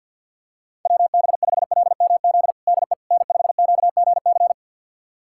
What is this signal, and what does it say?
Morse code, heard as a keyed tone, about 52 wpm, 700 Hz. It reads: WB5LMZ DE N5QGC
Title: USA Callsign DE Callsign 50wpm